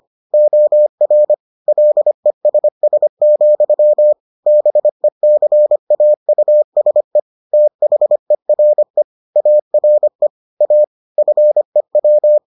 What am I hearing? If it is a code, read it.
Morse code, heard as a keyed tone, about 25 wpm, 600 Hz. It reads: OR LESS, BECAUSE THERE ARE A FEW